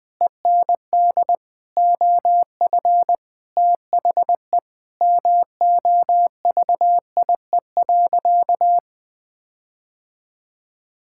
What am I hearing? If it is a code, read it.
Morse code, heard as a keyed tone, about 20 wpm, 700 Hz. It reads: END OF THE MOVIE.
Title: Sentences with 6-Word Phrases - 20wpm